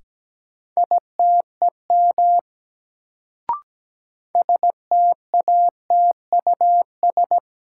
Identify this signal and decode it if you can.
Morse code, heard as a keyed tone, about 17 wpm, 700 Hz. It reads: ITEM STATUS
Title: Sentences from Top 1400 Words - New Word Drill Alt - 17wpm